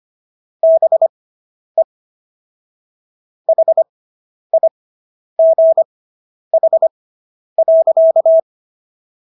Morse code, heard as a keyed tone, about 25 wpm, spaced as for 10 wpm, 650 Hz. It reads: BE HIGH.